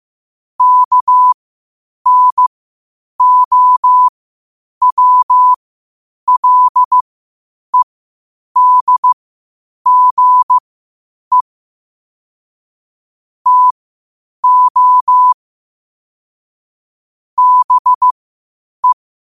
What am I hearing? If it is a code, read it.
Morse code, heard as a keyed tone, about 15 wpm, spaced as for 8 wpm, 1000 Hz. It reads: KNOWLEDGE TO BE
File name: qotdfiles / 8wpm